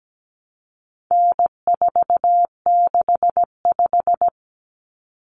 Morse code, heard as a keyed tone, about 17 wpm, 700 Hz. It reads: N465
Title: Common Cut Numbers - 4-Digit 17wpm